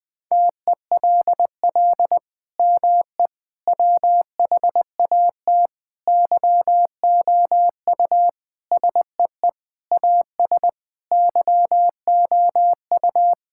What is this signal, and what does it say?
Morse code, heard as a keyed tone, about 20 wpm, 700 Hz. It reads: TELL ME WHAT YOU SEE AS YOU
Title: Sentences from Top 1600 Words 20wpm